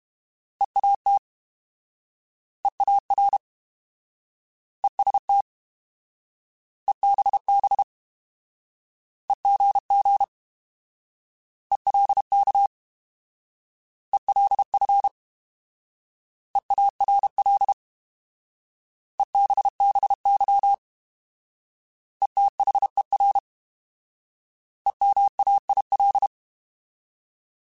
Morse code, heard as a keyed tone, about 32 wpm, 800 Hz. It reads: EAT EAR EST EBB EGG ELK ELF EARL EBBY ETHER EMAIL